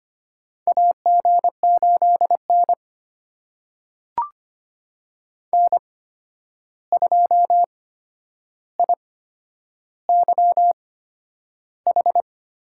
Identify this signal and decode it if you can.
Morse code, heard as a keyed tone, about 25 wpm, spaced as for 10 wpm, 700 Hz. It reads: AG8N N2IYH
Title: USA Call Signs - 8x Character Spacing - 25wpm